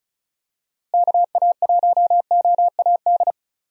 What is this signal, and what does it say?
Morse code, heard as a keyed tone, about 34 wpm, 700 Hz. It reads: KA1OAD
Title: USA Call Signs - 6 Characters - Repeated 3x - 35wpm